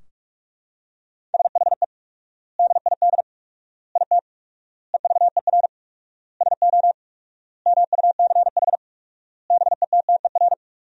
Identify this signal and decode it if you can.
Morse code, heard as a keyed tone, about 41 wpm, 700 Hz. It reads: SHE DID IT EVER SO MUCH BETTER
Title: 4x Word-Spacing - Sentences from Top 400 Words - 45wpm